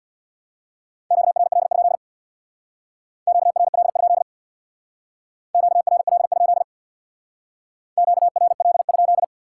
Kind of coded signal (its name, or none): Morse code